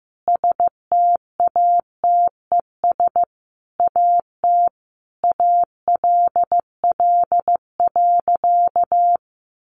Morse code, heard as a keyed tone, about 15 wpm, 700 Hz. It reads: STATES AT ALL.
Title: A Sound Of Thunder 15wpm